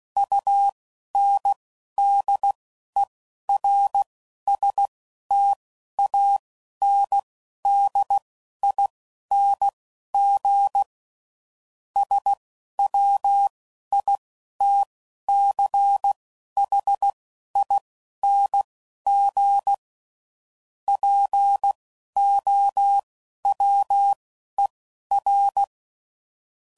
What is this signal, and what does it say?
Morse code, heard as a keyed tone, about 16 wpm, spaced as for 11 wpm, 800 Hz. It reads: UNDERSTANDING SWITCHING POWER
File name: CW Code Groups @10WPM / Technical Subjects / SwPwrSup